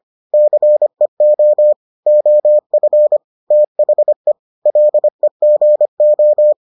Morse code, heard as a keyed tone, about 25 wpm, 600 Hz. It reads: CEO OF THE LEGO